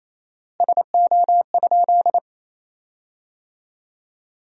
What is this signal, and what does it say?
Morse code, heard as a keyed tone, about 28 wpm, 700 Hz. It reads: SO?
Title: 4x Word-Spacing - Sentences from Top 200 Words - 28wpm